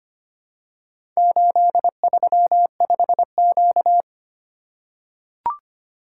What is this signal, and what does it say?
Morse code, heard as a keyed tone, about 25 wpm, 700 Hz. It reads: 835Q
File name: practice / 25wpm / Four Letter-Number Combinations - 25wpm